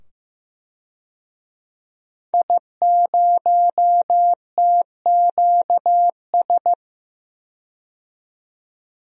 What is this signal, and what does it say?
Morse code, heard as a keyed tone, about 15 wpm, 700 Hz. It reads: I0TQS
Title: DXCC-100 Call Signs - 5 Characters - 15wpm